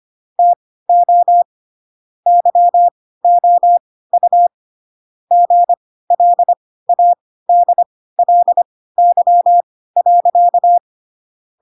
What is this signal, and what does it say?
Morse code, heard as a keyed tone, about 25 wpm, spaced as for 16 wpm, 700 Hz. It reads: TO YOU GLADLY.
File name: Pinocchio / Pinocchio ch7 10 25 ik1hge